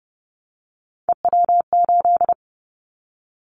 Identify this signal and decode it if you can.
Morse code, heard as a keyed tone, about 30 wpm, 700 Hz. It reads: EW8
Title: Callsign Prefixes for Top 50 DXCC - 30wpm